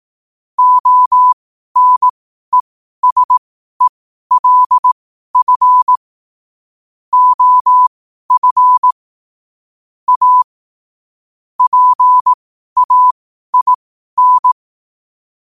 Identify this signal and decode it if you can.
Morse code, heard as a keyed tone, about 18 wpm, spaced as for 12 wpm, 1000 Hz. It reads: ONESELF OF A PAIN
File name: qotdfiles / 12wpm